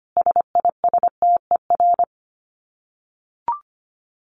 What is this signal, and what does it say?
Morse code, heard as a keyed tone, about 25 wpm, 700 Hz. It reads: SISTER
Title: Dit Challenge 25wpm